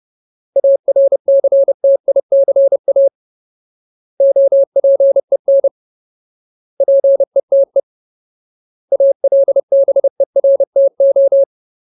Morse code, heard as a keyed tone, about 30 wpm, 550 Hz. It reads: ARCTICA OPEN PETE ALBERTO
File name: Ultimate Word List / 30WPM Ultimate